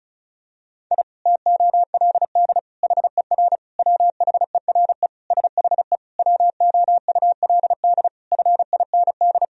Morse code, heard as a keyed tone, about 36 wpm, 700 Hz. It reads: I TOLD HER WHERE SHE WOULD FIND
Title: Sentences from Top 1000 Words - Review Part 2 - 35wpm